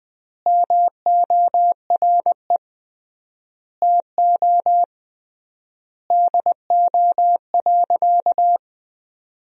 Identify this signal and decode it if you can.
Morse code, heard as a keyed tone, about 21 wpm, 700 Hz. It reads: MORE TO DO.
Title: 3x Word-Spacing - Sentences from Top 100 Words - 20wpm